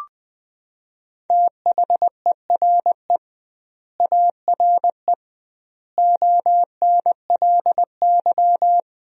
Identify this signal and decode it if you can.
Morse code, heard as a keyed tone, about 20 wpm, 700 Hz. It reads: THERE ARE ONLY
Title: 2x Word-Spacing - Sentences from Top 200 Words - 20wpm